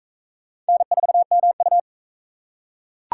Morse code, heard as a keyed tone, about 38 wpm, 700 Hz. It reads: N4MU